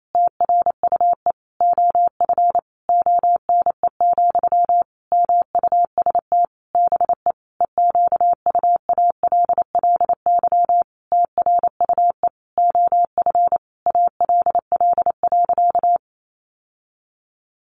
Morse code, heard as a keyed tone, about 28 wpm, 700 Hz. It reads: TRUE OF ONE, MUST BE EQUALLY TRUE OF ALL.
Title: Sentences from Top 1400 Words 28wpm